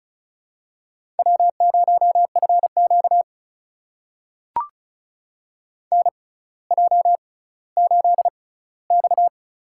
Morse code, heard as a keyed tone, about 36 wpm, spaced as for 18 wpm, 700 Hz. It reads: W0FQ NJ8X
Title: USA Call Signs - 4 Characters - 6x Character Spacing - 35wpm